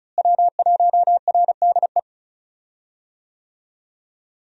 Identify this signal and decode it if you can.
Morse code, heard as a keyed tone, about 34 wpm, 700 Hz. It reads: W1RDE
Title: USA Call Signs - 5 Characters - Repeated 2x - 35wpm